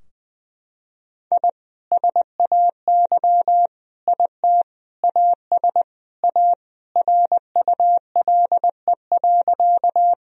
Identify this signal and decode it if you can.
Morse code, heard as a keyed tone, about 20 wpm, 700 Hz. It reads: I SAY IT AS A RULE.
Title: Sentences from Top 1000 Words - Review Part 2 - 20wpm